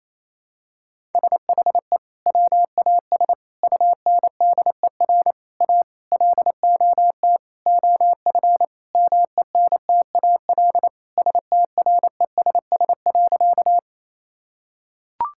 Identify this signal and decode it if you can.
Morse code, heard as a keyed tone, about 27 wpm, 700 Hz. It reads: SHE WAS UNDER A LOT OF MENTAL STRESS.
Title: Sentences from Top 1000 Words - Review Part 6 - 28wpm